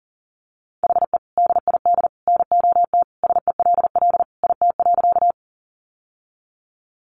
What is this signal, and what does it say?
Morse code, heard as a keyed tone, about 40 wpm, 700 Hz. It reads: HE DID NOT SELL IT.
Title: Sentences from Top 1100 Words 40wpm